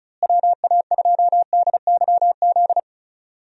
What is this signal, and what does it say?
Morse code, heard as a keyed tone, about 36 wpm, 700 Hz. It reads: WA2DYZ
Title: USA Call Signs - 6 Characters - Repeated 3x - 35wpm